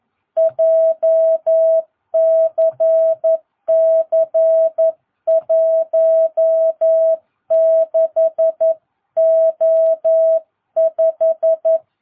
Morse code, heard as a keyed tone, about 11 wpm, 660 Hz. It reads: JCC 16O5
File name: jh8cbh / 2022cw